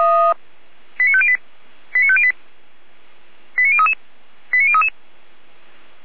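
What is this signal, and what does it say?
Five-tone selective call (ZVEI), heard as quick runs of five tones, tones of about 70 ms. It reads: 89508 89508 89033 89033
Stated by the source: ZVEI five-tone selective call, received on SDR.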